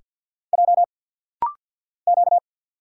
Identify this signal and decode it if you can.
Morse code, heard as a keyed tone, about 50 wpm, 690 Hz. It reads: J Q
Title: Single Letters - Rapid-Fire 50wpm